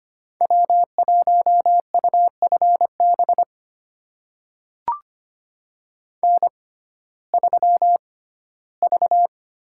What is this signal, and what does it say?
Morse code, heard as a keyed tone, about 25 wpm, spaced as for 15 wpm, 700 Hz. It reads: W1UFB N3V
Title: USA Call Signs - 5 Characters - 6x Character Spacing - 25wpm